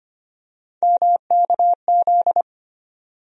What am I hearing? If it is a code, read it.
Morse code, heard as a keyed tone, about 25 wpm, 700 Hz. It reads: MKZ
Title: USA Call Sign Postfixes - 3 Letters 25wpm